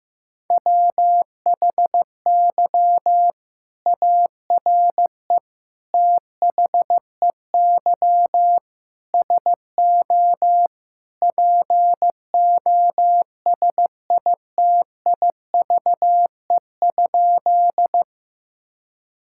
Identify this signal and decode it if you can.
Morse code, heard as a keyed tone, about 15 wpm, 700 Hz. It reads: WHY ARE THEY SO POSITIVE?
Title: Sentences from Top 1000 Words - Review Part 5 - 15wpm